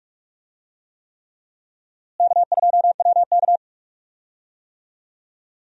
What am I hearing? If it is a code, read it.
Morse code, heard as a keyed tone, about 46 wpm, 700 Hz. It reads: K2WK
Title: USA Call Signs - 4 Characters - Repeated 3x - 45wpm